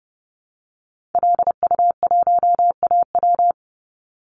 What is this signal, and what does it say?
Morse code, heard as a keyed tone, about 30 wpm, 700 Hz. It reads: LU1AW
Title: Call Signs Repeated 2x - 30wpm